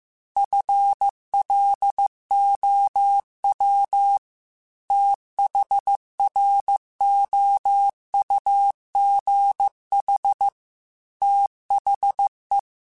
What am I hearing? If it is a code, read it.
Morse code, heard as a keyed tone, about 15 wpm, 800 Hz. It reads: FLOW THROUGH THE
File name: Technical Subjects / Technical Subjects-mp3s / Reactance @13